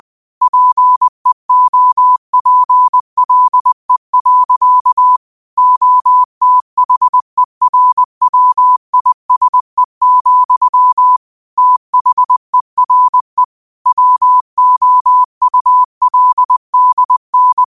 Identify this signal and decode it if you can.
Morse code, heard as a keyed tone, about 20 wpm, 1000 Hz. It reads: PEOPLE. OTHERWISE, THERE WOULDN
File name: qotdfiles / 20wpm